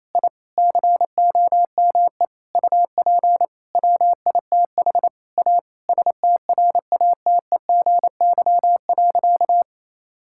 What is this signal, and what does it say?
Morse code, heard as a keyed tone, about 28 wpm, 700 Hz. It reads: I COME UP WITH A STRATEGY.
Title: Sentences from Top 1000 Words - Review Part 5 - 28wpm